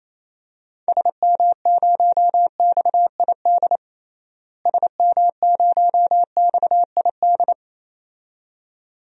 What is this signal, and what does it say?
Morse code, heard as a keyed tone, about 28 wpm, 700 Hz. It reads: SM0XID SM0XID
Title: DXCC-100 Call Signs - 6 Characters - Repeated 2x - 28wpm